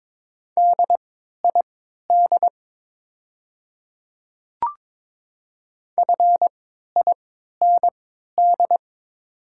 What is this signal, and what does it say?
Morse code, heard as a keyed tone, about 22 wpm, spaced as for 12 wpm, 690 Hz. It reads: DID FIND
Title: 3x Character Spacing - Top 100 Words - 22wpm